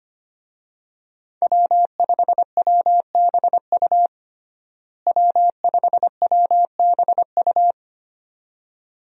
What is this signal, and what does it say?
Morse code, heard as a keyed tone, about 25 wpm, 700 Hz. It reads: W5WBU W5WBU
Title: USA Call Signs - 5 Characters - Repeated 2x - 25wpm